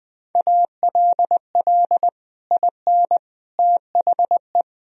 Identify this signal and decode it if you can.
Morse code, heard as a keyed tone, about 20 wpm, 700 Hz. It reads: ALL IN THE
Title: Sentences from Top 1000 Words - Review Part 2 - 20wpm